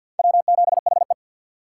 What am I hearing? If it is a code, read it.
Morse code, heard as a keyed tone, about 51 wpm, 700 Hz. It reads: W7HE
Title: USA Call Signs - 4 Characters - 3x Character Spacing - 50wpm